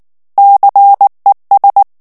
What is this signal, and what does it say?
Morse code, heard as a keyed tone, about 19 wpm, 800 Hz. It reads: CES
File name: FRANCAIS TEXTE CLAIR 2 VITESSE 18 MOTS MINUTES